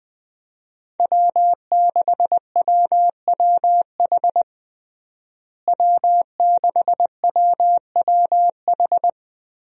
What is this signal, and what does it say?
Morse code, heard as a keyed tone, about 21 wpm, 700 Hz. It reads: W6WWH W6WWH
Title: USA Call Signs - 5 Characters - Repeated 3x - 20wpm